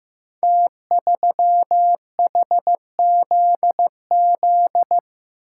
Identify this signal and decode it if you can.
Morse code, heard as a keyed tone, about 15 wpm, 700 Hz. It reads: T3HZZ
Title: Code Groups - 5 Letter:Numbers - 15wpm